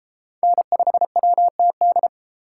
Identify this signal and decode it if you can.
Morse code, heard as a keyed tone, about 31 wpm, 700 Hz. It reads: N5WTD